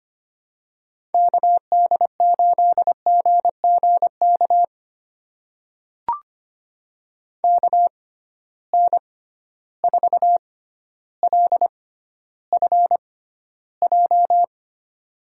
Morse code, heard as a keyed tone, about 25 wpm, spaced as for 13 wpm, 700 Hz. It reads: KD8GGK KN4LFJ
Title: USA Call Signs - 6 Characters - 6x Character Spacing - 25wpm